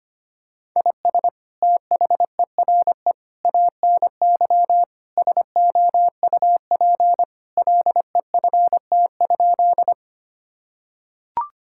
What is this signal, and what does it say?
Morse code, heard as a keyed tone, about 25 wpm, 700 Hz. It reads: IS THERE ANY SOUP LEFT?
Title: Sentences from Top 1000 Words - Review Part 6 - 25wpm